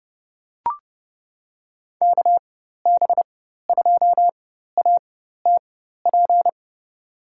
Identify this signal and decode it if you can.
Morse code, heard as a keyed tone, about 30 wpm, spaced as for 14 wpm, 700 Hz. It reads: KB2ATP
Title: USA Call Signs - 6 Characters - 4x Character Spacing - 30wpm